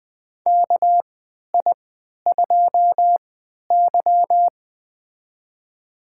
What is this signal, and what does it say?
Morse code, heard as a keyed tone, about 20 wpm, spaced as for 11 wpm, 700 Hz. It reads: KI2Y
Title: USA Call Signs - 4 Characters - 3x Character Spacing - 20wpm